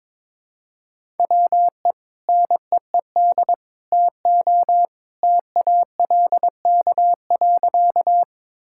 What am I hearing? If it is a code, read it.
Morse code, heard as a keyed tone, about 22 wpm, 700 Hz. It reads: WE NEED TO TALK.
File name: practice / 22wpm / Sentences with 2-Word Phrases - 22wpm